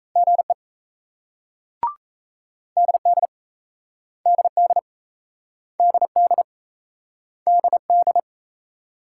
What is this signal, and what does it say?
Morse code, heard as a keyed tone, about 37 wpm, 700 Hz. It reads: GE DD DD DD DD